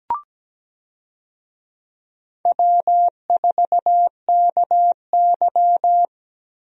Morse code, heard as a keyed tone, about 17 wpm, 700 Hz. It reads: W4KY
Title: USA Call Signs - 4 Characters - Repeated 2x - 17wpm